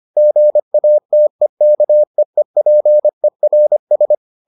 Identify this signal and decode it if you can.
Morse code, heard as a keyed tone, about 25 wpm, 600 Hz. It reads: GATEKEEPERS